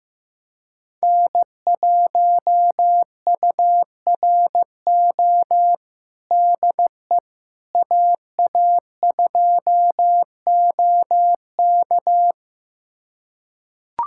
Morse code, heard as a keyed tone, about 15 wpm, 700 Hz. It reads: N1URO DE AA2OK
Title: USA Callsign DE Callsign - 5 Characters - 15wpm